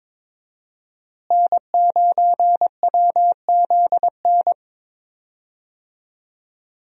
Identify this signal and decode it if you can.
Morse code, heard as a keyed tone, about 22 wpm, 700 Hz. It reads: N9WZN
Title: USA Call Signs - 5 Characters - Repeated 3x - 22wpm